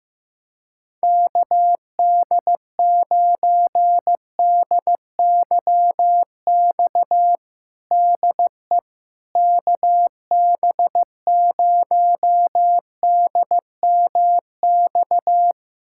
Morse code, heard as a keyed tone, about 16 wpm, 700 Hz. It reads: KD9DYX DE KB0DMX
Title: USA Callsign DE Callsign 15wpm